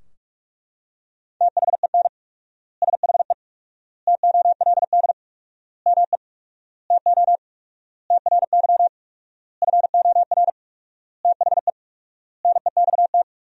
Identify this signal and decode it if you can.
Morse code, heard as a keyed tone, about 42 wpm, 700 Hz. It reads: THEN SHE TOLD ME TO TRY FOR THE NEXT
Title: 4x Word-Spacing - Sentences from Top 400 Words - 45wpm